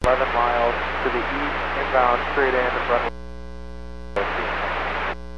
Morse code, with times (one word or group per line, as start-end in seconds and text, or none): none